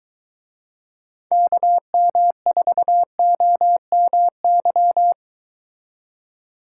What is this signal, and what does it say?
Morse code, heard as a keyed tone, about 23 wpm, 700 Hz. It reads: KM4OMY